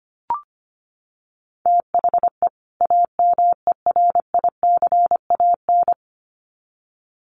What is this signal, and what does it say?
Morse code, heard as a keyed tone, about 25 wpm, 700 Hz. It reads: THE AMERICAN
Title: Sets of 2 Words 25wpm